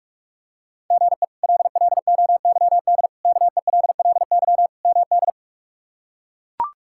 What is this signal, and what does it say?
Morse code, heard as a keyed tone, about 42 wpm, 700 Hz. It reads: GE LLOYD KELLY MD
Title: K1USN SST QSO Practice - 45wpm